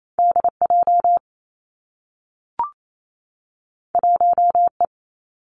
Morse code, heard as a keyed tone, about 28 wpm, 700 Hz. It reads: D J 1 E
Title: Two Challenging Letter-Number Combinations 28wpm